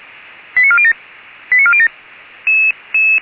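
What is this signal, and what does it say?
Five-tone selective call (ZVEI), heard as quick runs of five tones, tones of about 70 ms. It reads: 89407 89407
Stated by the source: ZVEI five-tone selective call received by SDR.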